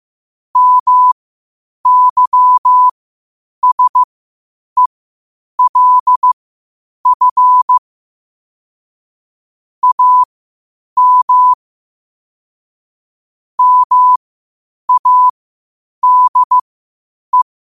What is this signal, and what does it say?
Morse code, heard as a keyed tone, about 15 wpm, spaced as for 8 wpm, 1000 Hz. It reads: MYSELF AM MADE
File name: qotdfiles / 8wpm